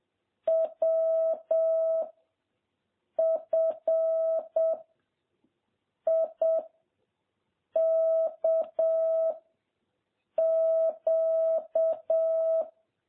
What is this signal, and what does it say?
Morse code, heard as a keyed tone, about 7 wpm, 660 Hz. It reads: W F I K Q
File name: jh8cbh / 2021cw